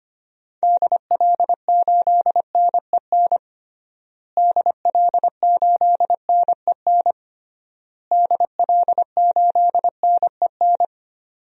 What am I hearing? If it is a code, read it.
Morse code, heard as a keyed tone, about 25 wpm, 700 Hz. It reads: DL8NEN DL8NEN DL8NEN
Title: DXCC-100 Call Signs - 6 Characters - Repeated 3x - 25wpm